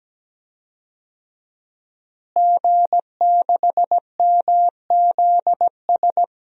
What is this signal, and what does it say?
Morse code, heard as a keyed tone, about 17 wpm, 700 Hz. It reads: G6MZS
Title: DXCC-100 Call Signs - 5 Characters - Repeated 2x - 17wpm